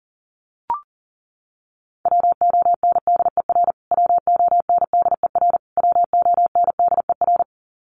Morse code, heard as a keyed tone, about 37 wpm, 700 Hz. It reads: WONDER WONDER WONDER
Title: Top 500 Words Repeated 40wpm